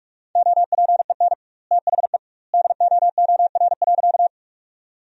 Morse code, heard as a keyed tone, about 44 wpm, 700 Hz. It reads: OPEN THE DOOR.
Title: 2x Word-Spacing - Sentences from Top 300 Words - 45wpm